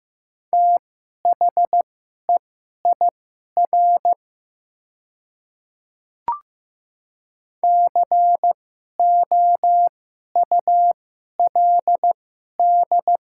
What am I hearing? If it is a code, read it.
Morse code, heard as a keyed tone, about 15 wpm, spaced as for 11 wpm, 700 Hz. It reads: THEIR COULD
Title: 2x Character Spacing - Top 500 Words - 15wpm